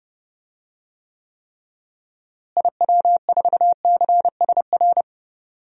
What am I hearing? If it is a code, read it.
Morse code, heard as a keyed tone, about 32 wpm, 700 Hz. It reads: IW4CSR
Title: DXCC-100 Call Signs - 6 Characters - Repeated 3x - 30wpm